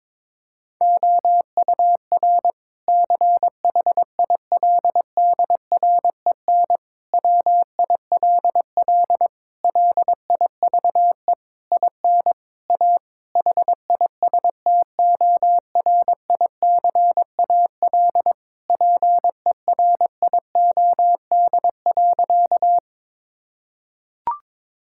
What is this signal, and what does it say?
Morse code, heard as a keyed tone, about 22 wpm, 700 Hz. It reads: OUR CHILDREN WILL LIVE IN A HISTORICAL PERIOD.
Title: Sentences from Top 1000 Words - Review Part 5 - 22wpm